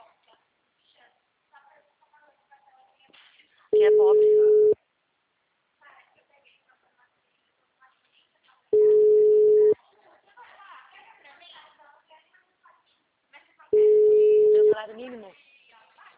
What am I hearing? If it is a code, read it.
Ringback tone (Europe, 425 Hz), with speech.